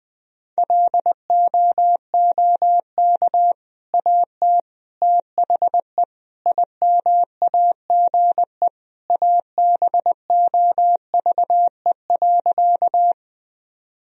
Morse code, heard as a keyed tone, about 20 wpm, 700 Hz. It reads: LOOK AT THE IMAGE ABOVE.
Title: Sentences from Top 1000 Words - Review Part 4 - 20wpm